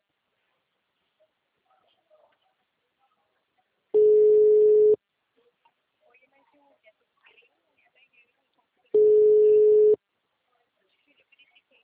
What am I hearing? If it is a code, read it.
Ringback tone (Europe, 425 Hz), with speech.